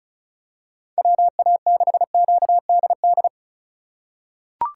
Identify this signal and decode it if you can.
Morse code, heard as a keyed tone, about 34 wpm, 700 Hz. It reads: WA6QDD
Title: USA Call Signs - 6 Characters - 2x Character Spacing - 35wpm